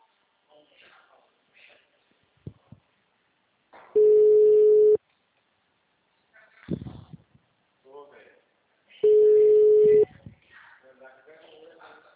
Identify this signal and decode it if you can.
Ringback tone (Europe, 425 Hz), with speech.